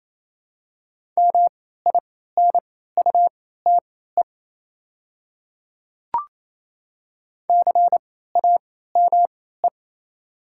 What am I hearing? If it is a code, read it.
Morse code, heard as a keyed tone, about 28 wpm, spaced as for 16 wpm, 700 Hz. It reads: MINUTE CAME
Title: 3x Character Spacing - Top 500 Words - 28wpm